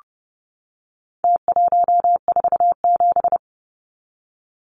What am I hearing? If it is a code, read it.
Morse code, heard as a keyed tone, about 30 wpm, 700 Hz. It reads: T147
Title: Numbers - Contest Style - 4-Digit 30wpm